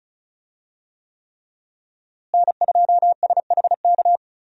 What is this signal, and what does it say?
Morse code, heard as a keyed tone, about 34 wpm, 700 Hz. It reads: N2SHK
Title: USA Call Signs - 5 Characters - Repeated 3x - 35wpm